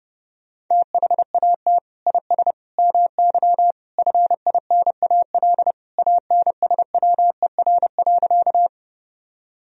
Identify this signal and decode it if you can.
Morse code, heard as a keyed tone, about 30 wpm, 700 Hz. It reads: THAT IS MY FINAL ANSWER.
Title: Sentences from Top 1000 Words - Review Part 2 - 30wpm